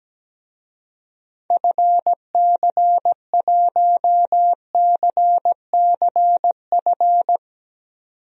Morse code, heard as a keyed tone, about 17 wpm, 700 Hz. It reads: FC1CCF
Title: DXCC-100 Call Signs - 6 Characters - 17wpm